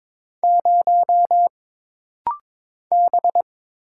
Morse code, heard as a keyed tone, about 22 wpm, 700 Hz. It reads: 0 B
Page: This rapid-fire timing is geared to push your ICR (Instant Character Recognition) skills.